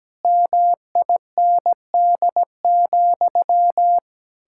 Morse code, heard as a keyed tone, about 17 wpm, 700 Hz. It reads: MIND,
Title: Sentences from Top 1000 Words - Review Part 6 - 17wpm